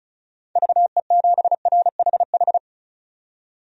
Morse code, heard as a keyed tone, about 34 wpm, 700 Hz. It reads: VE7RHH
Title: DXCC-100 Call Signs - 6 Characters - 35wpm